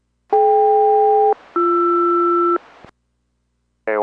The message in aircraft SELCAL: DK-BR